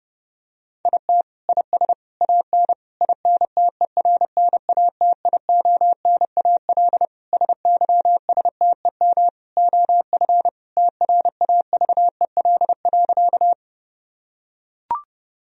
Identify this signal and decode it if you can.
Morse code, heard as a keyed tone, about 32 wpm, 700 Hz. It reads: IT IS AN INTERNATIONAL SYSTEM OF TRAVEL.
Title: Sentences from Top 1000 Words - Review Part 4 - 30wpm